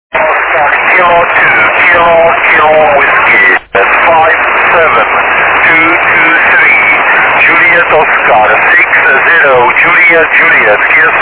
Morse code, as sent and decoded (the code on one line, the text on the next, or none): none